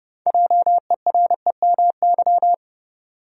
Morse code, heard as a keyed tone, about 30 wpm, 700 Hz. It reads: JEREMY
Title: USA Common Names - 30wpm